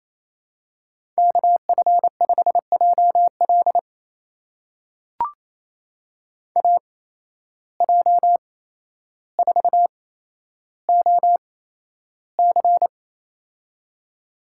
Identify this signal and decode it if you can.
Morse code, heard as a keyed tone, about 27 wpm, spaced as for 12 wpm, 700 Hz. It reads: KF5JL AJ4OC
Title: USA Call Signs - 5 Characters - 8x Character Spacing - 28wpm